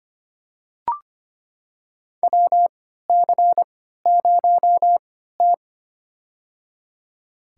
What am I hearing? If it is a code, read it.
Morse code, heard as a keyed tone, about 25 wpm, spaced as for 14 wpm, 700 Hz. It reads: WC0T
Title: USA Call Signs - 4 Characters - 3x Character Spacing - 25wpm